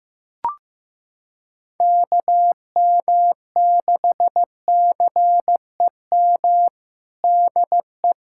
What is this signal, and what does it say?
Morse code, heard as a keyed tone, about 15 wpm, 700 Hz. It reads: KM6CEM DE
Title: USA Callsign DE Callsign - 6 Characters - 15wpm